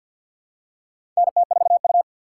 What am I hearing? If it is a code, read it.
Morse code, heard as a keyed tone, about 51 wpm, 700 Hz. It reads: NT4U
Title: USA Call Signs - 4 Characters - 2x Character Spacing - 50wpm